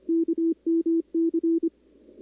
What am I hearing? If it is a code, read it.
Morse code, heard as a keyed tone, about 30 wpm, 340 Hz. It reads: KMC
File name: contests / 2019-cw-wpx-kt5j-k5pi- -2330 0-11